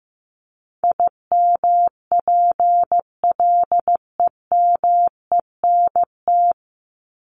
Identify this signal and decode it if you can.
Morse code, heard as a keyed tone, about 15 wpm, 700 Hz. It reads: IMPLEMENT
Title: Top 1501-1750 Words 15wpm